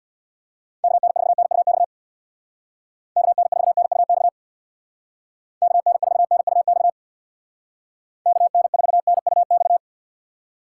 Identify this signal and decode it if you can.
Morse code, heard as a keyed tone, about 60 wpm, 700 Hz. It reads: KN4NUX KN4NUX KN4NUX KN4NUX
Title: Speed Racing - USA 2x3 Call Signs 50wpm